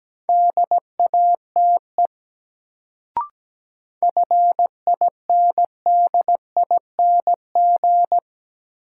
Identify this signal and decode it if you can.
Morse code, heard as a keyed tone, about 17 wpm, 700 Hz. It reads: DATE FINDING
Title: Sentences from Top 1100 Words - New Word Drill Alt - 17wpm